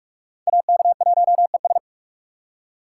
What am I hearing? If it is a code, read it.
Morse code, heard as a keyed tone, about 42 wpm, 700 Hz. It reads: AK1ES